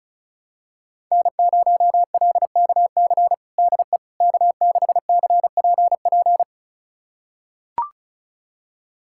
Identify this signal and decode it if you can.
Morse code, heard as a keyed tone, about 36 wpm, 700 Hz. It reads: N0LKC DE K6CPP
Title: USA Callsign DE Callsign - 5 Characters - 35wpm